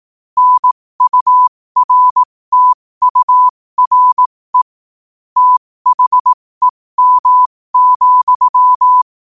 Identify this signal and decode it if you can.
Morse code, heard as a keyed tone, about 18 wpm, spaced as for 15 wpm, 1000 Hz. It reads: NURTURE THEM,
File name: qotdfiles / 15wpm